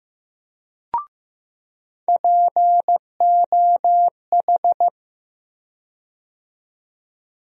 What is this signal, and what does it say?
Morse code, heard as a keyed tone, about 15 wpm, 700 Hz. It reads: POH
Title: USA Call Sign Postfixes - 3 Letters 15wpm